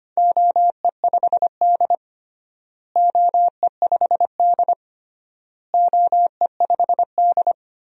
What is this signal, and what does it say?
Morse code, heard as a keyed tone, about 25 wpm, 700 Hz. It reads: OE5D OE5D OE5D
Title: DXCC-100 Call Signs - 4 Characters - Repeated 3x - 25wpm